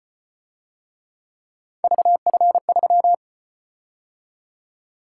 Morse code, heard as a keyed tone, about 34 wpm, 700 Hz. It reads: VF3